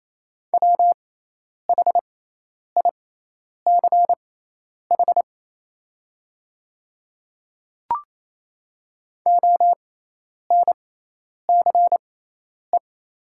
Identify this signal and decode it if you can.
Morse code, heard as a keyed tone, about 28 wpm, spaced as for 10 wpm, 700 Hz. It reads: WHICH ONCE
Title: 6x Character Spacing - Top 300 Words - 28wpm